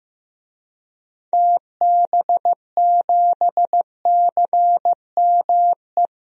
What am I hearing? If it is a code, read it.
Morse code, heard as a keyed tone, about 15 wpm, 700 Hz. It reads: TB7CME
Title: DXCC-100 Call Signs - 6 Characters - 15wpm